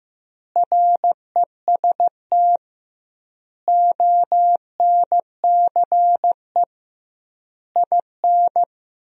Morse code, heard as a keyed tone, about 16 wpm, 700 Hz. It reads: REST ONCE IN
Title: 2x Word-Spacing - Sentences from Top 500 Words - 15wpm